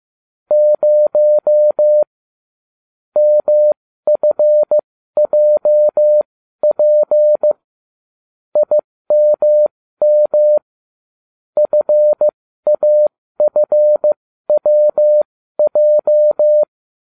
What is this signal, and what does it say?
Morse code, heard as a keyed tone, about 15 wpm, spaced as for 13 wpm, 600 Hz. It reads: 0 MFJP IMM FAFWJ